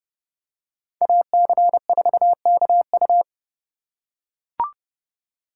Morse code, heard as a keyed tone, about 29 wpm, 700 Hz. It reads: AC4KU